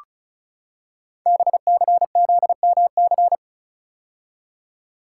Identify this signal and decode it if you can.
Morse code, heard as a keyed tone, about 36 wpm, 690 Hz. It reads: BCZMC
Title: Code Groups - 5 Letters - 35wpm